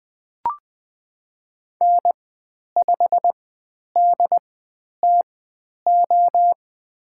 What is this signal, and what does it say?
Morse code, heard as a keyed tone, about 20 wpm, spaced as for 10 wpm, 700 Hz. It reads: N5DTO
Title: 095 Lesson 16g - Letter c - Call Signs 20wpm